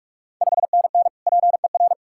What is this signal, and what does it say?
Morse code, heard as a keyed tone, about 46 wpm, 700 Hz. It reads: 5NN PER
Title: Ontario QSO Party Exchange 45wpm